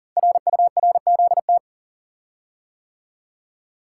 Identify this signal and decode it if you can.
Morse code, heard as a keyed tone, about 40 wpm, 700 Hz. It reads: RURZT
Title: Code Groups - 5 Letters - 40wpm